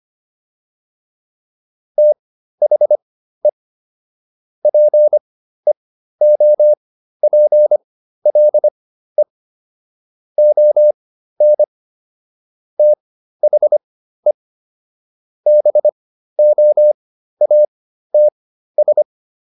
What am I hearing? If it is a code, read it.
Morse code, heard as a keyed tone, about 25 wpm, spaced as for 13 wpm, 600 Hz. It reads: THE PEOPLE ON THE BOATS